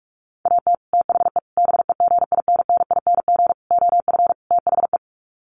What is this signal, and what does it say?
Morse code, heard as a keyed tone, about 44 wpm, 700 Hz. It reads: AT THE BEGINNING OF THE
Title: Sets of 5 Words 45wpm